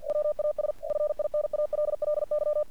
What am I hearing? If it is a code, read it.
Morse code, heard as a keyed tone, about 47 wpm, 600 Hz. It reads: WAS FINALLY
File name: archived / CRSnet